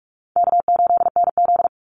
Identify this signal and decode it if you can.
Morse code, heard as a keyed tone, about 44 wpm, 700 Hz. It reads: K8NZ
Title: Easy Call Signs - 45wpm